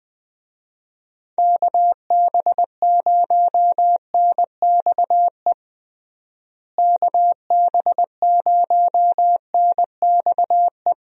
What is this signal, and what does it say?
Morse code, heard as a keyed tone, about 20 wpm, 700 Hz. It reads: KB0NXE KB0NXE
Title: USA Call Signs - 6 Characters - Repeated 3x - 20wpm